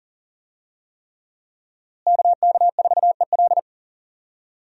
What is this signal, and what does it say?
Morse code, heard as a keyed tone, about 37 wpm, 700 Hz. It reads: KK4EL
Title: USA Call Signs - 5 Characters - Repeated 3x - 40wpm